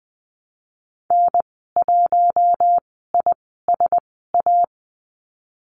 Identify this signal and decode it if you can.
Morse code, heard as a keyed tone, about 19 wpm, spaced as for 14 wpm, 700 Hz. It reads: N1ISA
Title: 062 Lesson 11g - Letter h - Call Signs 20wpm